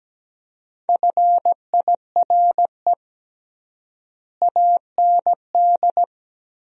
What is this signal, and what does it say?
Morse code, heard as a keyed tone, about 17 wpm, 700 Hz. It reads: FIRE AND
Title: 3x Word-Spacing - Sentences from Top 400 Words - 17wpm